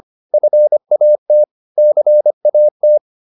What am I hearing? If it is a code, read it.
Morse code, heard as a keyed tone, about 25 wpm, 600 Hz. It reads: FAT CAT
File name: PR204a 25